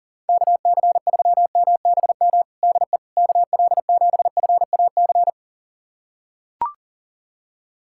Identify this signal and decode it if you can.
Morse code, heard as a keyed tone, about 37 wpm, 700 Hz. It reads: KC3MBM DE KL7FAC
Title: USA Callsign DE Callsign 40wpm